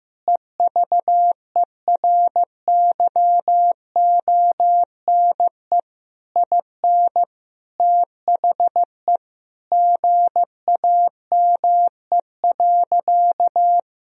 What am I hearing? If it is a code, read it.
Morse code, heard as a keyed tone, about 15 wpm, 700 Hz. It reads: EVERYONE IN THE GAME.
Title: Sentences from Top 1000 Words - Review Part 4 - 15wpm